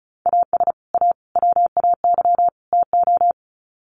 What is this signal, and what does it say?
Morse code, heard as a keyed tone, about 34 wpm, 700 Hz. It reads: AS A WAY TO
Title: Sets of 4 Words 35wpm